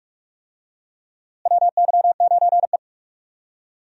Morse code, heard as a keyed tone, about 42 wpm, 700 Hz. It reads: WY9E
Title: USA Call Signs - 4 Characters - Repeated 2x - 45wpm